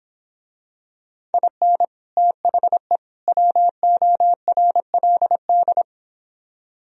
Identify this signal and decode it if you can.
Morse code, heard as a keyed tone, about 25 wpm, 700 Hz. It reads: IN THE WORLD